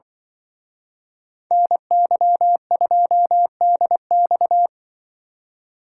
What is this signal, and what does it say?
Morse code, heard as a keyed tone, about 24 wpm, 700 Hz. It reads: NY2DX
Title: Speed Racing - USA 2x2 Call Signs 20wpm